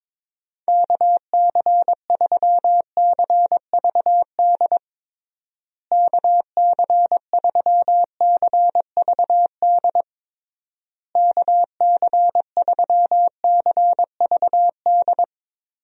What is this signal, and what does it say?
Morse code, heard as a keyed tone, about 22 wpm, 700 Hz. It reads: KC3CVD KC3CVD KC3CVD
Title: USA Call Signs - 6 Characters - Repeated 3x - 22wpm